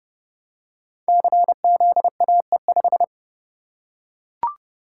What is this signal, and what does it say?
Morse code, heard as a keyed tone, about 30 wpm, 700 Hz. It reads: CZAE5
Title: Code Groups - 5 Letter:Numbers - 30wpm